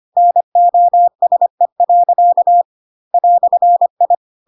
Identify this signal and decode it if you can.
Morse code, heard as a keyed tone, about 25 wpm, 700 Hz. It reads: NOSE. "I
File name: Pinocchio / Pinocchio ch3 25 25 ik1hge